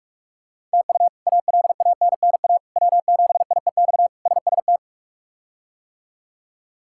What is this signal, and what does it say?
Morse code, heard as a keyed tone, about 43 wpm, 700 Hz. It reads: TU ALANNA W7IEX SST
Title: K1USN SST QSO Practice - 45wpm